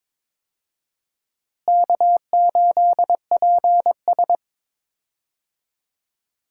Morse code, heard as a keyed tone, about 22 wpm, 700 Hz. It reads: K8PS
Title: USA Call Signs - 4 Characters - Repeated 3x - 22wpm